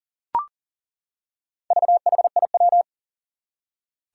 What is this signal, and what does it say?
Morse code, heard as a keyed tone, about 37 wpm, 700 Hz. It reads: VHIW
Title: Four Challenging Letter-Number Combinations 40wpm